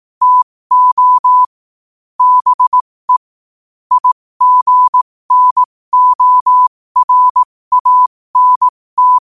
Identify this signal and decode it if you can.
Morse code, heard as a keyed tone, about 18 wpm, spaced as for 15 wpm, 1000 Hz. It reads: TO BE IGNORANT
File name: qotdfiles / 15wpm